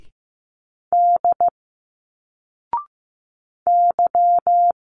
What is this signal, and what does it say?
Morse code, heard as a keyed tone, about 15 wpm, 700 Hz. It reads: D Y
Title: Single Letters 15wpm